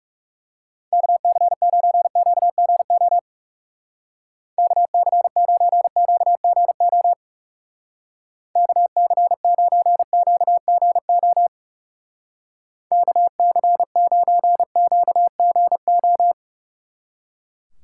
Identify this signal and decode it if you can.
Morse code, heard as a keyed tone, about 38 wpm, 700 Hz. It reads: KC9QGO KC9QGO KC9QGO KC9QGO